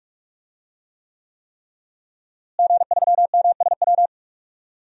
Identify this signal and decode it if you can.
Morse code, heard as a keyed tone, about 46 wpm, 700 Hz. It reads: G3MSW
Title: DXCC-100 Call Signs - 5 Characters - Repeated 3x - 45wpm